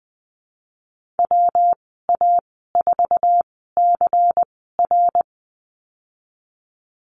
Morse code, heard as a keyed tone, about 19 wpm, spaced as for 14 wpm, 700 Hz. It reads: WA4CR
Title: 140 Lesson 23g - Letter f - Call Signs 20wpm